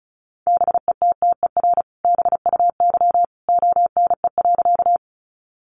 Morse code, heard as a keyed tone, about 34 wpm, 700 Hz. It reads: BETTER BUY ONE.
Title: Sentences from Top 700 Words 35wpm